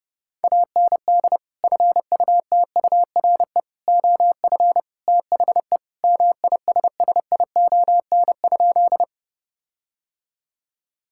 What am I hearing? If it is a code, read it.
Morse code, heard as a keyed tone, about 30 wpm, 700 Hz. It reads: AND FUTURE OF THE MISSION?
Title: Sentences from Top 1600 Words 30wpm